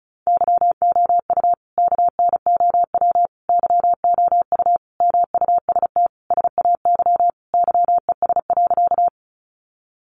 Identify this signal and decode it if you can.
Morse code, heard as a keyed tone, about 34 wpm, 700 Hz. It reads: YOU KNOW YOU MUST SAY YES.
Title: Sentences from Top 500 Words 35wpm